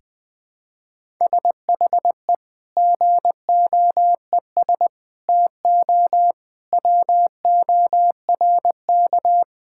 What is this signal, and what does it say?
Morse code, heard as a keyed tone, about 21 wpm, 700 Hz. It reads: SHE GOES TO WORK
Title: Sentences with 2-Word Phrases - 20wpm